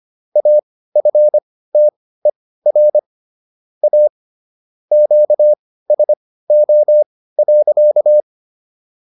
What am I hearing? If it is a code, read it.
Morse code, heard as a keyed tone, about 25 wpm, spaced as for 16 wpm, 600 Hz. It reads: AFTER A QSO.